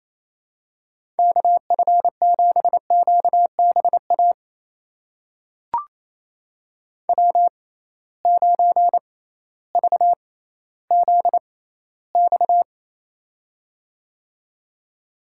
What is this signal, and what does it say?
Morse code, heard as a keyed tone, about 28 wpm, spaced as for 15 wpm, 700 Hz. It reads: KF7QBA W9VZX
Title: USA Call Signs - 6x Character Spacing - 28wpm